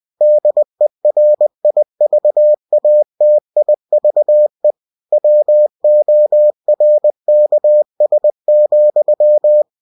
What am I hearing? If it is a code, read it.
Morse code, heard as a keyed tone, about 20 wpm, 600 Hz. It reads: DERIVATIVE WORKS,
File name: cw-books / winnie-the-pooh / winnie-20wpm-page-0112